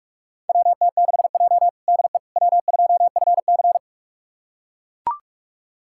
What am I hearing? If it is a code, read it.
Morse code, heard as a keyed tone, about 41 wpm, 700 Hz. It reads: WT6J DE W2FC
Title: USA Callsign DE Callsign - 4 Characters - 45wpm